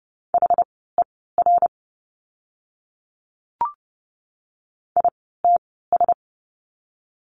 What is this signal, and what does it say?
Morse code, heard as a keyed tone, about 30 wpm, spaced as for 17 wpm, 700 Hz. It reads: HER ITS
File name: course / 30:10wpm / 230 Lesson 37f - Letter x - Words 30wpm